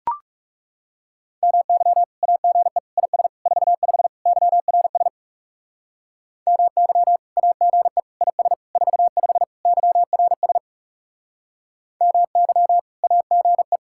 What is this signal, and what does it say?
Morse code, heard as a keyed tone, about 38 wpm, 700 Hz. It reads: MY AGE IS 45 YRS MY AGE IS 45 YRS MY AGE